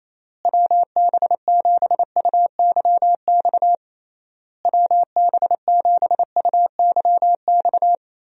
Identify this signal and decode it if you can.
Morse code, heard as a keyed tone, about 28 wpm, 700 Hz. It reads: WB7UYX WB7UYX
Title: USA Call Signs - 6 Characters - Repeated 3x - 28wpm